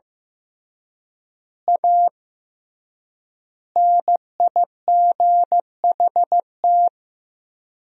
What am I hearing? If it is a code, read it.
Morse code, heard as a keyed tone, about 15 wpm, 700 Hz. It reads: A NIGHT
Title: 3x Word-Spacing - Sentences from Top 300 Words - 15wpm